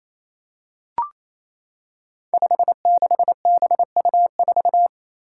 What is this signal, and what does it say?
Morse code, heard as a keyed tone, about 28 wpm, 700 Hz. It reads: 56BU4
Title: Five Challenging Letter-Number Combinations 28wpm